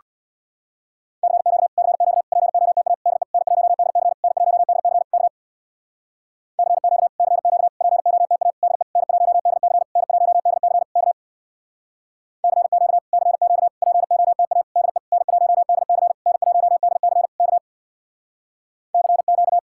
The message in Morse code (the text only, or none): CQ CQ POTA DE N1DC N1DC K CQ CQ POTA DE N1DC N1DC K CQ CQ POTA DE N1DC N1DC K CQ